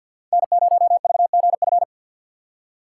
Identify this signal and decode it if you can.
Morse code, heard as a keyed tone, about 52 wpm, 690 Hz. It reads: N0VGF
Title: USA Call Signs - 2x Character Spacing - 50wpm